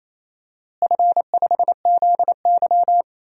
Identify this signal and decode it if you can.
Morse code, heard as a keyed tone, about 28 wpm, 700 Hz. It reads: F5ZY
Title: DXCC-100 Call Signs - 4 Characters - Repeated 2x - 28wpm